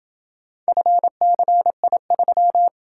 Morse code, heard as a keyed tone, about 28 wpm, 700 Hz. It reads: FCI3